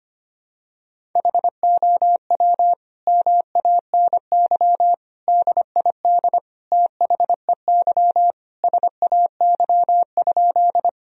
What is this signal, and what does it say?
Morse code, heard as a keyed tone, about 25 wpm, 700 Hz. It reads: HOW MANY DID THEY SAY?
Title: Sentences from Top 1000 Words - Review Part 1 - 25wpm